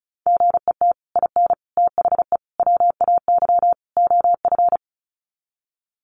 Morse code, heard as a keyed tone, about 34 wpm, 700 Hz. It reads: GET IN THE WAY OF
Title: Sets of 5 Words Encore 35wpm